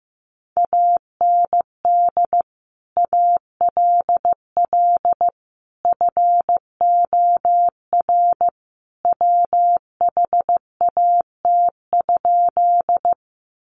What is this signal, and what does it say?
Morse code, heard as a keyed tone, about 15 wpm, 700 Hz. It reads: AND ALL FOR WHAT?
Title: Sentences from Top 100 Words 15wpm